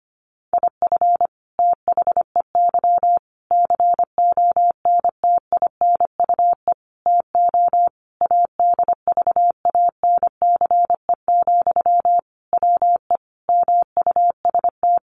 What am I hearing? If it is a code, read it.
Morse code, heard as a keyed tone, about 25 wpm, 700 Hz. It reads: IF THEY CONTINUE TO ADVANCE, WE MUST
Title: Sentences from Top 1400 Words 25wpm